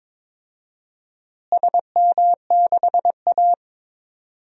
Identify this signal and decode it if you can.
Morse code, heard as a keyed tone, about 22 wpm, 700 Hz. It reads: SM6A
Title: DXCC-100 Call Signs - 4 Characters - Repeated 3x - 22wpm